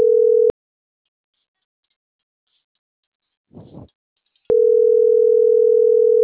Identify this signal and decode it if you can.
Ringback tone (North America), 440 + 480 Hz.